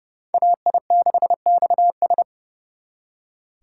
Morse code, heard as a keyed tone, about 39 wpm, 700 Hz. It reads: AI6XS